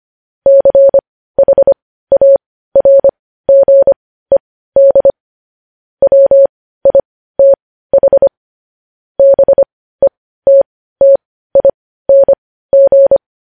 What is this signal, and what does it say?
Morse code, heard as a keyed tone, about 25 wpm, spaced as for 15 wpm, 550 Hz. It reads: CHARGED WITH BETTING
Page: News Headlines in Morse Code at 15 WPM